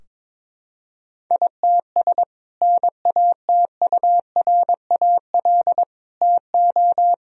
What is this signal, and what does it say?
Morse code, heard as a keyed tone, about 22 wpm, 700 Hz. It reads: ITS NATURAL TO
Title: Sentences from Top 1000 Words - Review Part 4 - 22wpm